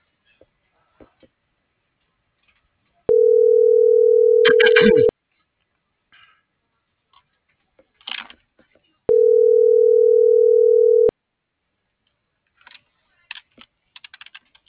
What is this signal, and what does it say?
Ringback tone (North America), 440 + 480 Hz.
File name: CENTRAL MAYAB / out jorge.armando